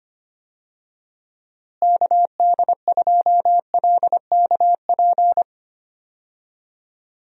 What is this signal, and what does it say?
Morse code, heard as a keyed tone, about 25 wpm, 700 Hz. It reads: KD2LKP
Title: USA Call Signs - 6 Characters - Repeated 3x - 25wpm